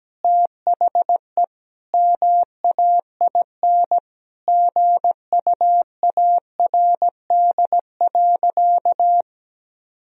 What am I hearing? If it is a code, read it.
Morse code, heard as a keyed tone, about 17 wpm, 700 Hz. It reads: THE MAIN GUARD.
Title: Sentences from Top 1000 Words - Review Part 4 - 17wpm